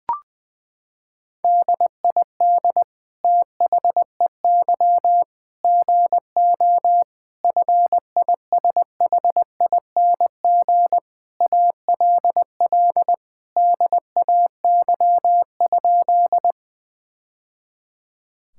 Morse code, heard as a keyed tone, about 21 wpm, 700 Hz. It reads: DID THEY GO FISHING ALL DAY?
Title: Sentences from Top 1000 Words - Review Part 5 - 20wpm